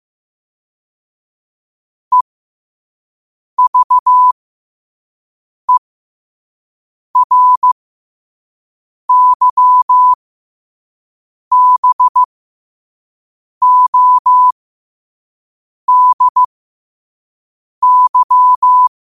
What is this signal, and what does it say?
Morse code, heard as a keyed tone, about 15 wpm, spaced as for 5 wpm, 1000 Hz. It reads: EVERYBODY
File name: qotdfiles / 5wpm